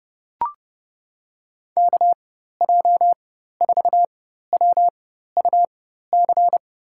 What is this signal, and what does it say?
Morse code, heard as a keyed tone, about 30 wpm, spaced as for 14 wpm, 700 Hz. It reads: KJ4WUC
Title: USA Call Signs - 6 Characters - 4x Character Spacing - 30wpm